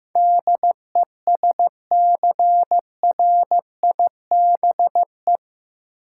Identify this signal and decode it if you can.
Morse code, heard as a keyed tone, about 15 wpm, 700 Hz. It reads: DESCRIBE
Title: Sentences from Top 1000 Words - New Word Drill - 15wpm